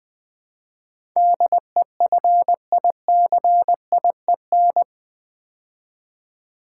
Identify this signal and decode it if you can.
Morse code, heard as a keyed tone, about 20 wpm, 700 Hz. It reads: DEFICIEN
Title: Build A Word - 4 Letters At A Time - 10-Letter Words - 20wpm